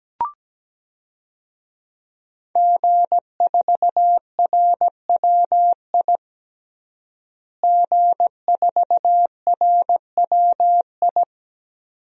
Morse code, heard as a keyed tone, about 17 wpm, 700 Hz. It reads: G4RWI G4RWI
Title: DXCC-100 Call Signs - 5 Characters - Repeated 2x - 17wpm